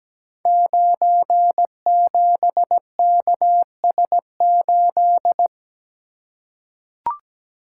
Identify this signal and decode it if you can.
Morse code, heard as a keyed tone, about 17 wpm, 700 Hz. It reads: 97KS8
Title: Code Groups - 5 Letter:Numbers - 17wpm